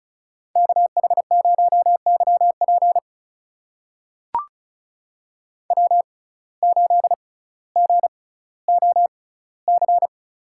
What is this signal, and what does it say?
Morse code, heard as a keyed tone, about 36 wpm, spaced as for 18 wpm, 700 Hz. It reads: KH0YP W8GOC
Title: USA Call Signs - 5 Characters - 6x Character Spacing - 35wpm